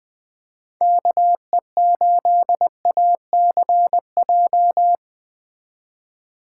Morse code, heard as a keyed tone, about 20 wpm, 700 Hz. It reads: KE8ACJ
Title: USA Call Signs - 6x Character Spacing - 20wpm